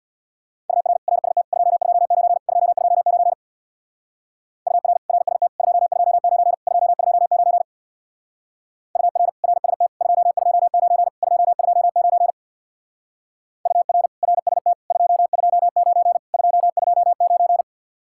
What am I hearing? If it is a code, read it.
Morse code, heard as a keyed tone, about 59 wpm, 700 Hz. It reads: UR RST 229 229 UR RST 229 229 UR RST 229 229 UR RST 229 229
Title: Speed-Racing Common QSO Phrases - 50wpm